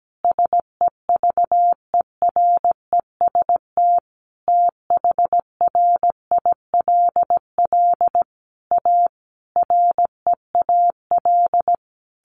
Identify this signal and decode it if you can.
Morse code, heard as a keyed tone, about 17 wpm, 700 Hz. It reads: SEVEREST THRILL A REAL
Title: A Sound Of Thunder 17wpm